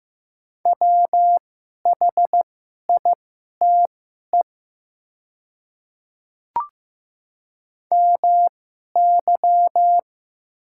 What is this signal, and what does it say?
Morse code, heard as a keyed tone, about 15 wpm, spaced as for 11 wpm, 700 Hz. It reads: WHITE MY
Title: 2x Character Spacing - Top 300 Words - 15wpm